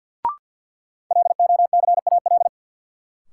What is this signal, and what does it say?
Morse code, heard as a keyed tone, about 52 wpm, 700 Hz. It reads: LOCAL
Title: Sentences from Top 1500 Words - New Word Drill Alt - 50wpm